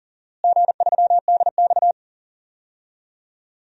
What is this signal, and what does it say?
Morse code, heard as a keyed tone, about 40 wpm, 690 Hz. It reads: G3DX
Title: DXCC-100 Call Signs - 4 Characters - Repeated 2x - 40wpm